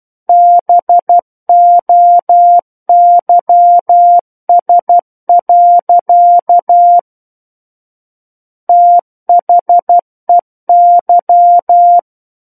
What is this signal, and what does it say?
Morse code, heard as a keyed tone, about 12 wpm, 700 Hz. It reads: BOYS. THEY
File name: Book One / 03-Chapter 3 - 12 WPM